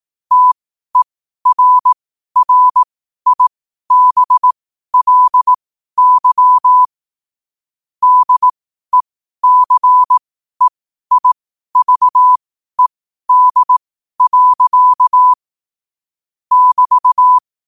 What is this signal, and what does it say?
Morse code, heard as a keyed tone, about 18 wpm, spaced as for 12 wpm, 1000 Hz. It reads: TERRIBLY DECEIVED. <BT>
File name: qotdfiles / 12wpm